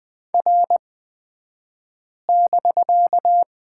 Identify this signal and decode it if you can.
Morse code, heard as a keyed tone, about 20 wpm, spaced as for 5 wpm, 700 Hz. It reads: R<BK>
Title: 249 Lesson 40d - BK - Two Characters 20wpm